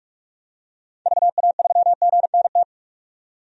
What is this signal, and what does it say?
Morse code, heard as a keyed tone, about 43 wpm, 700 Hz. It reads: VA3GNT
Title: DXCC-100 Call Signs - 6 Characters - 45wpm